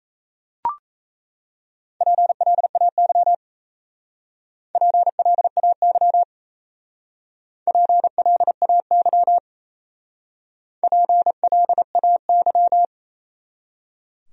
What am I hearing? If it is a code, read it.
Morse code, heard as a keyed tone, about 35 wpm, 700 Hz. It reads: PLAY PLAY PLAY PLAY